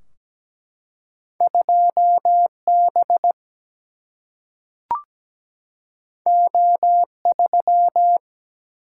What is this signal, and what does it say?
Morse code, heard as a keyed tone, about 17 wpm, 700 Hz. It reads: 2B O3
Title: Single-Letter Single-Number 17wpm